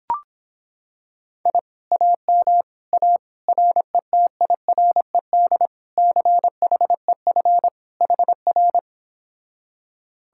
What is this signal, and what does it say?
Morse code, heard as a keyed tone, about 26 wpm, 700 Hz. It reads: I AM A RETIRED CHEF HR